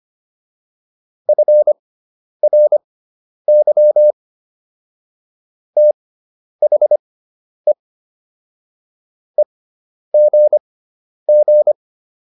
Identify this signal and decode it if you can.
Morse code, heard as a keyed tone, about 25 wpm, spaced as for 10 wpm, 600 Hz. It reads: FRY THE EGG